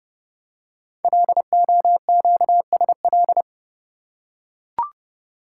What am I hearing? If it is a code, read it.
Morse code, heard as a keyed tone, about 30 wpm, 700 Hz. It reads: LOQSL
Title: Code Groups - 5 Letters - 30wpm